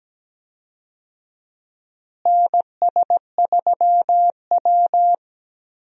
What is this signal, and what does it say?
Morse code, heard as a keyed tone, about 17 wpm, 700 Hz. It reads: NS3W
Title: USA Call Signs - 4 Characters - Repeated 3x - 17wpm